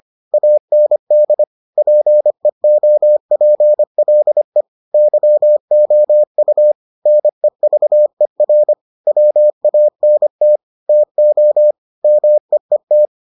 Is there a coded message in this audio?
Yes: Morse code from 0.3 to 13.1 s, heard as a keyed tone, about 25 wpm, 600 Hz. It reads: AND PEOPLE YOU NEVER WANT TO MEET